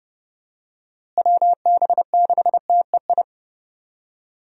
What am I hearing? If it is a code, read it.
Morse code, heard as a keyed tone, about 30 wpm, 700 Hz. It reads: WB6TEI
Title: USA Call Signs - 6 Characters - Repeated 2x - 30wpm